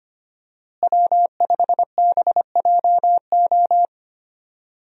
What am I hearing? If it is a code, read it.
Morse code, heard as a keyed tone, about 25 wpm, 700 Hz. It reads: W5BJO
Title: USA Call Signs - 5 Characters - Repeated 3x - 25wpm